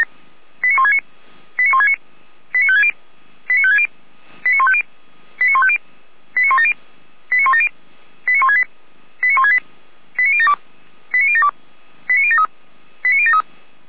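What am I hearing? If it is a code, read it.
Five-tone selective call (ZVEI), heard as quick runs of five tones, tones of about 70 ms. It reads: 89169 89169 89560 89560 89150 89150 89170 89170 89168 89168 89072 89072 89073 89073
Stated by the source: ZVEI five-tone call 89223 received on an SDR, channel 455, from Feuerwehr Orlamünde.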